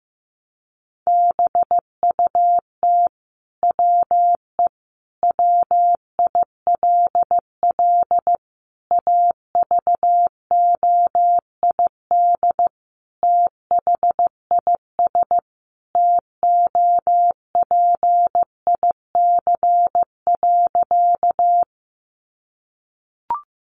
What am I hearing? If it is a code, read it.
Morse code, heard as a keyed tone, about 15 wpm, 700 Hz. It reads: BUT WE WILL AVOID THIS TOPIC.
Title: Sentences from Top 1200 Words 15wpm